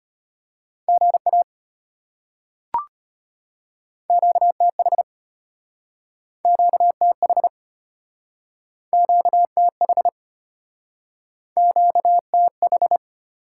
Morse code, heard as a keyed tone, about 34 wpm, 690 Hz. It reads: GA QTH QTH QTH QTH